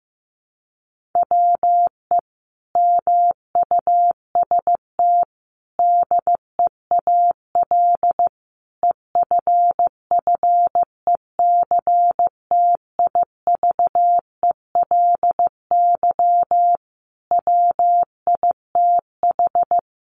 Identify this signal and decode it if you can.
Morse code, heard as a keyed tone, about 15 wpm, 700 Hz. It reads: WE MUST DEAL EFFECTIVELY WITH
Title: Sentences from Top 1400 Words 15wpm